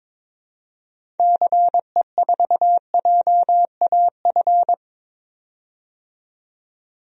Morse code, heard as a keyed tone, about 22 wpm, 700 Hz. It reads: CE4JAF
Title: DXCC-100 Call Signs - 6 Characters - 22wpm